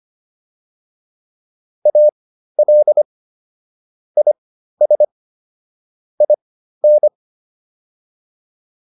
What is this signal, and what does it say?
Morse code, heard as a keyed tone, about 25 wpm, spaced as for 13 wpm, 600 Hz. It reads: AL IS IN